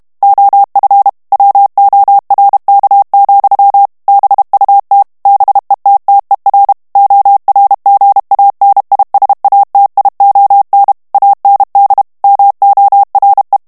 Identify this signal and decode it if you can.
Morse code, heard as a keyed tone, about 32 wpm, 800 Hz. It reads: OF WORK, BUT BETTER ORGANISATION AND MORE
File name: morse mp3 / ANGLAIS texte clair 1 vitesse 30 mots minute